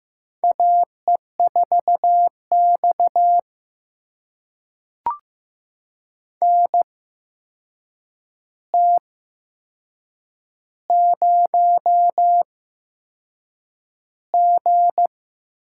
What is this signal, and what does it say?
Morse code, heard as a keyed tone, about 15 wpm, spaced as for 6 wpm, 700 Hz. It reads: AE4X NT0G
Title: USA Call Signs - 4 Characters - 8x Character Spacing - 15wpm